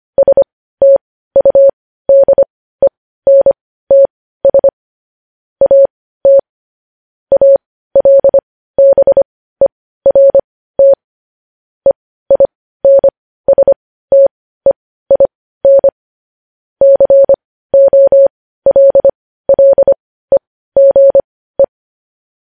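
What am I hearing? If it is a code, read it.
Morse code, heard as a keyed tone, about 25 wpm, spaced as for 15 wpm, 550 Hz. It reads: STUDENTS AT ALBERT EINSTEIN COLLEGE
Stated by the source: Morse news headlines practice, possibly sent at 15 wpm.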